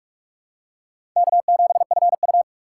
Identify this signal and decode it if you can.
Morse code, heard as a keyed tone, about 46 wpm, 700 Hz. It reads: K7FU
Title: USA Call Signs - 4 Characters - 4x Character Spacing - 45wpm